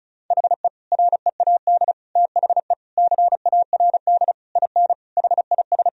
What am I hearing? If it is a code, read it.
Morse code, heard as a keyed tone, about 34 wpm, 700 Hz. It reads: HE READ THE CARD IN HIS
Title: Sentences from Top 1000 Words - Review Part 4 - 35wpm